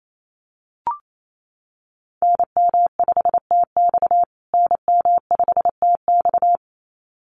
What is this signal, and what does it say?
Morse code, heard as a keyed tone, about 28 wpm, 700 Hz. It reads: NM5TX NM5TX
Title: USA Call Signs Repeated 2x 28wpm